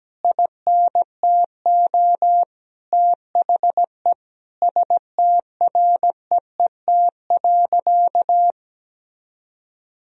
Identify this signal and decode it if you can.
Morse code, heard as a keyed tone, about 17 wpm, 700 Hz. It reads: INTO THE STREET.
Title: Sentences from Top 1000 Words - Review Part 3 - 17wpm